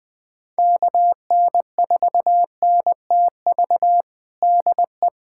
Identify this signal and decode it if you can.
Morse code, heard as a keyed tone, about 20 wpm, 700 Hz. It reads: KN4NTV DE
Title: USA Callsign DE Callsign 20wpm